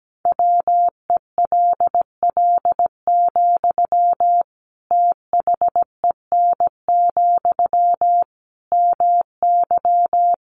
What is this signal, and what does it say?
Morse code, heard as a keyed tone, about 17 wpm, 700 Hz. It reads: WELL, THEN, MY
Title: Sentences from Top 1400 Words 17wpm